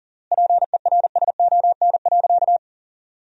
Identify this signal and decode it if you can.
Morse code, heard as a keyed tone, about 40 wpm, 700 Hz. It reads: PERSON.
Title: 3x Word-Spacing - Sentences from Top 500 Words - 40wpm